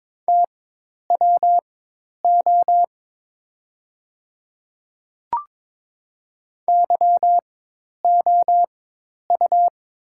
Morse code, heard as a keyed tone, about 22 wpm, spaced as for 10 wpm, 700 Hz. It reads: TWO YOU
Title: 4x Character Spacing - Top 100 Words - 22wpm